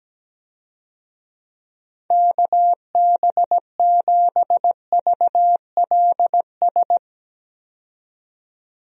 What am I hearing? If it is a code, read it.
Morse code, heard as a keyed tone, about 17 wpm, 700 Hz. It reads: KB7VLS
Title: USA Call Signs - 6 Characters - 17wpm